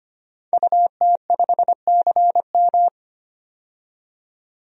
Morse code, heard as a keyed tone, about 25 wpm, 700 Hz. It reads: UT5CM
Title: DXCC-100 Call Signs - 5 Characters - 25wpm